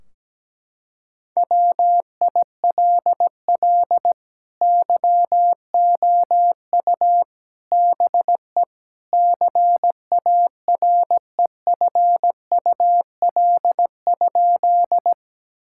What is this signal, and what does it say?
Morse code, heard as a keyed tone, about 17 wpm, 700 Hz. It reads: WILL YOU BE CAREFUL?
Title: Sentences from Top 1600 Words 17wpm